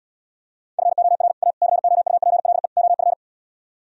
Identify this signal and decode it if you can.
Morse code, heard as a keyed tone, about 77 wpm, 700 Hz. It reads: FOR A COUPLE OF